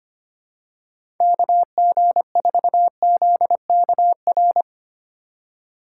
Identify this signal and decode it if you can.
Morse code, heard as a keyed tone, about 25 wpm, 700 Hz. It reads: KG4ZKR
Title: USA Call Signs - 6 Characters - 3x Character Spacing - 25wpm